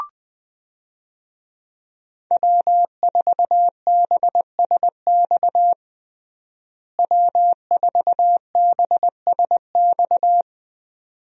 Morse code, heard as a keyed tone, about 20 wpm, 700 Hz. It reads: W4BSX W4BSX
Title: USA Call Signs - 5 Characters - Repeated 2x - 20wpm